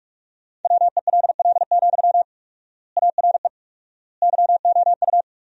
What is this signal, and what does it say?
Morse code, heard as a keyed tone, about 43 wpm, 700 Hz. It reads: WELL, ARE YOU
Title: 4x Word-Spacing - Sentences from Top 400 Words - 45wpm